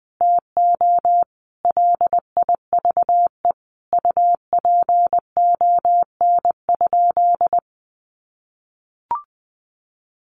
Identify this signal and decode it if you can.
Morse code, heard as a keyed tone, about 20 wpm, 700 Hz. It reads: TO LIVE UPON?
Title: Sentences from Top 1100 Words 20wpm